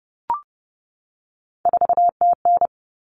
Morse code, heard as a keyed tone, about 30 wpm, 700 Hz. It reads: 4TN
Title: All Cut Numbers - 3-Digit 30wpm